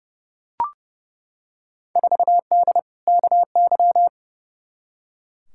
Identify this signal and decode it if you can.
Morse code, heard as a keyed tone, about 31 wpm, 700 Hz. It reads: 4D KY
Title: Field Day Exchanges Encore - 30wpm